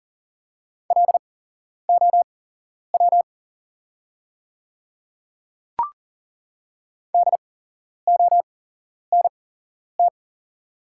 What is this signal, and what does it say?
Morse code, heard as a keyed tone, about 40 wpm, spaced as for 11 wpm, 700 Hz. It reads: LOW DONT
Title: 8x Character Spacing - Top 300 Words - 40wpm